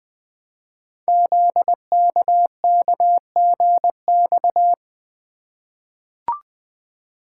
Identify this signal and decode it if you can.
Morse code, heard as a keyed tone, about 20 wpm, 700 Hz. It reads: ZKKGX
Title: Code Groups - 5 Letter:Numbers - 20wpm